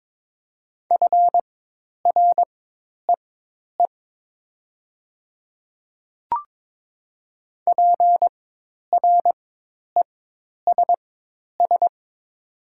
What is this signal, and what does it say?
Morse code, heard as a keyed tone, about 22 wpm, spaced as for 10 wpm, 700 Hz. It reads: FREE PRESS
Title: 4x Character Spacing - Top 500 Words - 22wpm